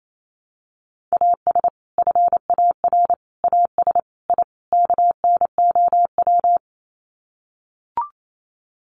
Morse code, heard as a keyed tone, about 28 wpm, 700 Hz. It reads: AS FAR AS I KNOW
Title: Sets of 5 Words 28wpm